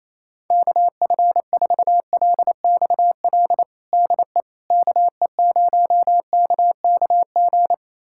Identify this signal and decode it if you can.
Morse code, heard as a keyed tone, about 28 wpm, 700 Hz. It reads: KF4LXL DE KE0KKG
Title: USA Callsign DE Callsign - 6 Characters - 28wpm